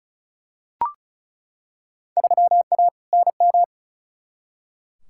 Morse code, heard as a keyed tone, about 34 wpm, 700 Hz. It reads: 3A NM
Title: Field Day Exchanges Encore - 35wpm